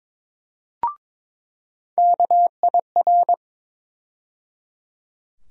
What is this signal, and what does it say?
Morse code, heard as a keyed tone, about 22 wpm, 700 Hz. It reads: KIR
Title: USA Call Sign Postfixes - 3 Letters 22wpm